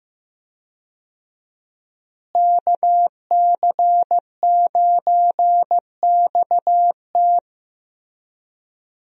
Morse code, heard as a keyed tone, about 15 wpm, 700 Hz. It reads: KC9XT
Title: USA Call Signs - 5 Characters - Repeated 2x - 15wpm